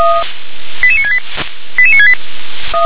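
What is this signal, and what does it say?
Five-tone selective call (ZVEI), heard as quick runs of five tones, tones of about 80 ms. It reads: 88968 88968
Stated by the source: ZVEI five-tone selective call received by SDR.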